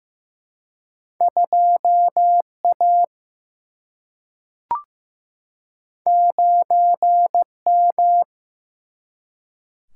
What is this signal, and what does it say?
Morse code, heard as a keyed tone, about 15 wpm, 700 Hz. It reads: 2 A 9 M
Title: Single-Letter Single-Number 15wpm